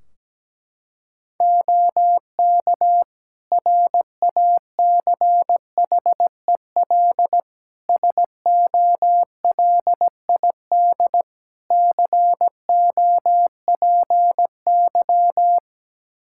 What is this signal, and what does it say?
Morse code, heard as a keyed tone, about 17 wpm, 700 Hz. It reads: OK RACHEL SOLID COPY
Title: K1USN SST QSO Practice 17wpm